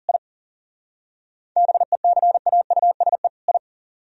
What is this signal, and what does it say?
Morse code, heard as a keyed tone, about 48 wpm, 690 Hz. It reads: I BECAUSE I
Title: Speed-Racing - Sets of 2 Words 40wpm